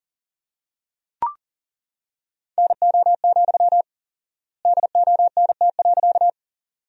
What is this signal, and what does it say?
Morse code, heard as a keyed tone, about 40 wpm, 700 Hz. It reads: NO, DONT.
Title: 4x Word-Spacing - Sentences from Top 300 Words - 40wpm